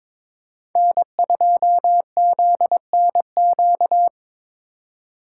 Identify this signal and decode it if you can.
Morse code, heard as a keyed tone, about 26 wpm, 700 Hz. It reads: N2ZNQ
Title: Speed Racing - USA Call Signs 22wpm